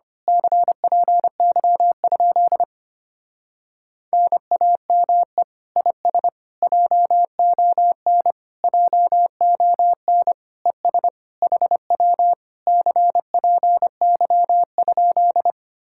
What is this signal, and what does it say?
Morse code, heard as a keyed tone, about 24 wpm, 700 Hz. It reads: CPY? NAME IS JON JON ES HW CPY?
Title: Speed-Racing Common QSO Phrases Encore - 25wpm